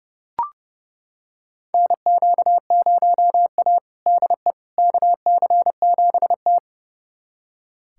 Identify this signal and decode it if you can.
Morse code, heard as a keyed tone, about 32 wpm, 700 Hz. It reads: NQ0A DE KC7T
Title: USA Callsign DE Callsign - 4 Characters - 30wpm